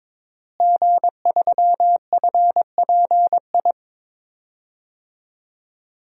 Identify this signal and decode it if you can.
Morse code, heard as a keyed tone, about 22 wpm, 700 Hz. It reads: G3FPI
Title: DXCC-100 Call Signs - 5 Characters - Repeated 2x - 22wpm